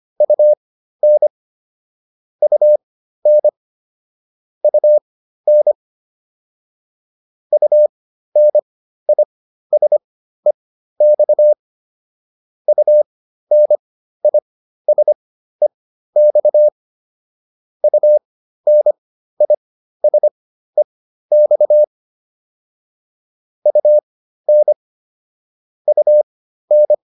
Morse code, heard as a keyed tone, about 25 wpm, spaced as for 13 wpm, 600 Hz. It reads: UN UN UN UNISEX UNISEX UNISEX UN UN